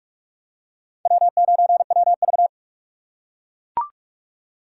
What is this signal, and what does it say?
Morse code, heard as a keyed tone, about 41 wpm, 700 Hz. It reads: W9WV